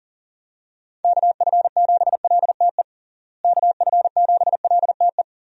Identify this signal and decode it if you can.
Morse code, heard as a keyed tone, about 40 wpm, 700 Hz. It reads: KF7LTE KF7LTE
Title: USA Call Signs - 6 Characters - Repeated 2x - 40wpm